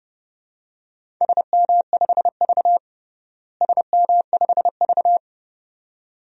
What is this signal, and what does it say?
Morse code, heard as a keyed tone, about 30 wpm, 700 Hz. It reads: SM5V SM5V
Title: DXCC-100 Call Signs - 4 Characters - Repeated 2x - 30wpm